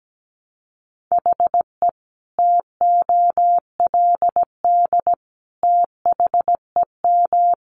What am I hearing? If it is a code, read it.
Morse code, heard as a keyed tone, about 17 wpm, 700 Hz. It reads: HE TOLD THEM
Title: Sentences from Top 700 Words 17wpm